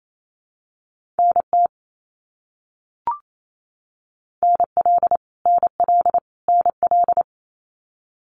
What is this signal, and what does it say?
Morse code, heard as a keyed tone, about 28 wpm, 700 Hz. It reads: NT NL NL NL
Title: Canadian Provinces and Territories - Repeated 3x - 28wpm